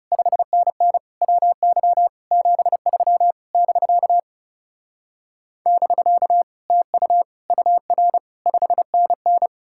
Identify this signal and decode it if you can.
Morse code, heard as a keyed tone, about 40 wpm, 700 Hz. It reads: 5NN WY 73 <BK> <BK> TU UR 5NN